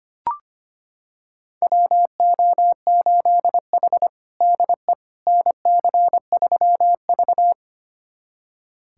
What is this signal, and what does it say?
Morse code, heard as a keyed tone, about 25 wpm, 700 Hz. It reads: WO8H DE NC3V
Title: USA Callsign DE Callsign - 4 Characters - 25wpm